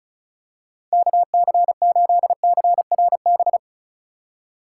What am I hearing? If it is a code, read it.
Morse code, heard as a keyed tone, about 36 wpm, 700 Hz. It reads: KC8CRB
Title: USA Call Signs - 6 Characters - 2x Character Spacing - 35wpm